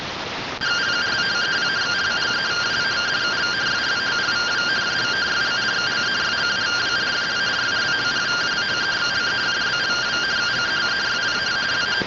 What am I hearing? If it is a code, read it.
Two-tone teleprinter signal, about 170 Hz shift.